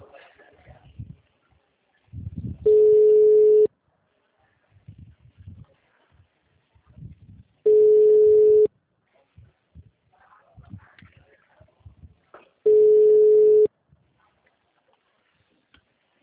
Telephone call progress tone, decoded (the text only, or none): ringback tone